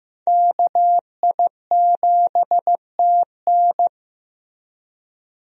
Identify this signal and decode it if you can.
Morse code, heard as a keyed tone, about 15 wpm, 700 Hz. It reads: KI7TN
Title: USA Call Signs - 5 Characters - Repeated 2x - 15wpm